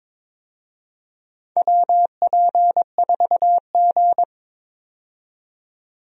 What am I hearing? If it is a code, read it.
Morse code, heard as a keyed tone, about 22 wpm, 700 Hz. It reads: WP4G
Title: USA Call Signs - 4 Characters - 22wpm